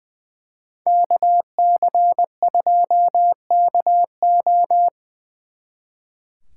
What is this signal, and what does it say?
Morse code, heard as a keyed tone, about 25 wpm, 700 Hz. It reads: KC2KO
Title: Speed Racing - USA 2x2 Call Signs 20wpm